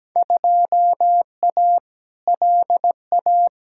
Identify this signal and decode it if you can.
Morse code, heard as a keyed tone, about 17 wpm, 700 Hz. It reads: 2A LA
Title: Field Day Exchanges - 17wpm